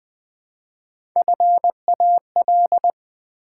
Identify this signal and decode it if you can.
Morse code, heard as a keyed tone, about 20 wpm, 700 Hz. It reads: FAL
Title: Build A Word - 1 Letter At A Time - 5-Letter Words - 20wpm